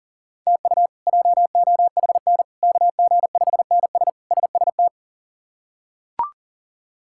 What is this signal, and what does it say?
Morse code, heard as a keyed tone, about 37 wpm, 700 Hz. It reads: TU JOHN KG5NS SST
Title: K1USN SST QSO Practice - 40wpm